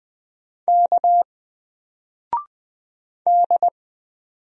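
Morse code, heard as a keyed tone, about 20 wpm, 700 Hz. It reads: K D
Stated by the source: Single letter and number Morse drill at 20 wpm, from N4LCD.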